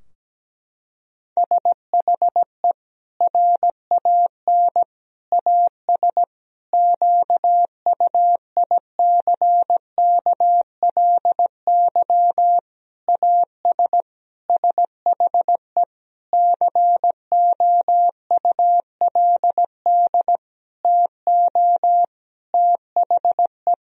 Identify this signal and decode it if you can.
Morse code, heard as a keyed tone, about 17 wpm, 700 Hz. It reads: SHE RAN AS QUICKLY AS SHE COULD TO THE
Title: Sentences from Top 1000 Words - Review Part 4 - 17wpm